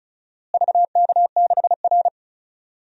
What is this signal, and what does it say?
Morse code, heard as a keyed tone, about 36 wpm, 700 Hz. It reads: VK6R
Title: DXCC-100 Call Signs - 4 Characters - Repeated 2x - 35wpm